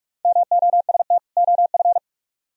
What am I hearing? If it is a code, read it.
Morse code, heard as a keyed tone, about 57 wpm, 700 Hz. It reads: MOST OF